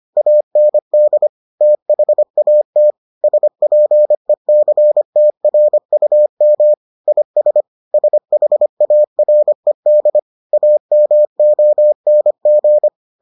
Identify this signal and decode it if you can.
Morse code, heard as a keyed tone, about 25 wpm, 600 Hz. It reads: AND THAT SPECTRUM IS SHARED AMONG